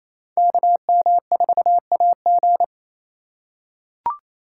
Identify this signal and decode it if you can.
Morse code, heard as a keyed tone, about 28 wpm, 700 Hz. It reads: KM4AG
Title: USA Call Signs - 2x Character Spacing - 28wpm